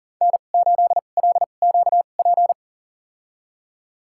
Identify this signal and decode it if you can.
Morse code, heard as a keyed tone, about 38 wpm, spaced as for 28 wpm, 700 Hz. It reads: N8LQP
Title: USA Call Signs - 5 Characters - 2x Character Spacing - 40wpm